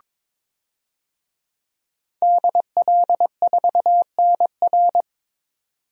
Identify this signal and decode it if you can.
Morse code, heard as a keyed tone, about 22 wpm, 700 Hz. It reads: DL4NR
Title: DXCC-100 Call Signs - 5 Characters - 22wpm